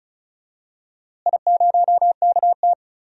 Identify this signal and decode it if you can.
Morse code, heard as a keyed tone, about 36 wpm, 700 Hz. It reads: I0KT
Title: DXCC-100 Call Signs - 4 Characters - Repeated 3x - 35wpm